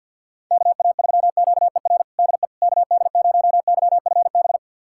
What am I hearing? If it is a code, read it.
Morse code, heard as a keyed tone, about 46 wpm, 700 Hz. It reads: KA3QER DE KD0YFB
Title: USA Callsign DE Callsign - 6 Characters - 50wpm